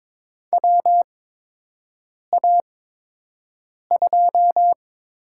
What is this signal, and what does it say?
Morse code, heard as a keyed tone, about 22 wpm, spaced as for 6 wpm, 700 Hz. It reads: WA2
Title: USA Call Signs - 6 Characters - 8x Character Spacing - 22wpm